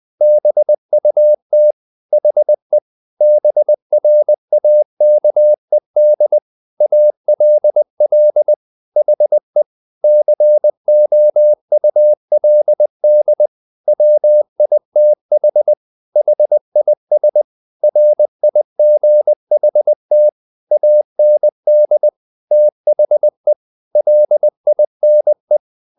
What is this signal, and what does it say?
Morse code, heard as a keyed tone, about 20 wpm, 600 Hz. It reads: BUT HE BRAKED ALL HE COULD WITH HIS RIGHT AND THE LINE
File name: the old man and the sea part 2 20wpm